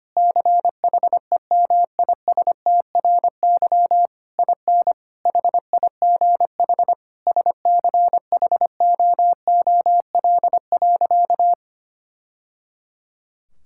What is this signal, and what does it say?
Morse code, heard as a keyed tone, about 25 wpm, 700 Hz. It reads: CHEMISTRY IN HIGH SCHOOL.
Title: Sentences from Top 1000 Words - Review Part 6 - 25wpm